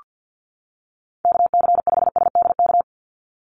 Morse code, heard as a keyed tone, about 48 wpm, 700 Hz. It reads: KC5SDK
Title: USA Call Signs 50wpm